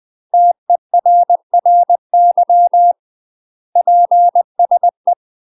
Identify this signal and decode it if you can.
Morse code, heard as a keyed tone, about 20 wpm, 700 Hz. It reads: TERRY PSE
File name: UK MASTER QSOs TRAINING / 20WPM / UK MASTER QSOs 0022